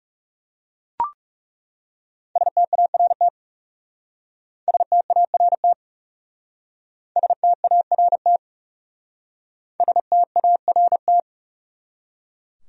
Morse code, heard as a keyed tone, about 37 wpm, 700 Hz. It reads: START START START START
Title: Speed-Racing Top 500 Words 30wpm